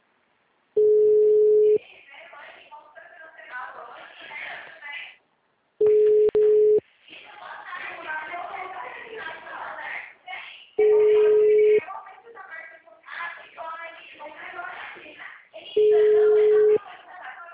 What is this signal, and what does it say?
Ringback tone (Europe, 425 Hz), with speech.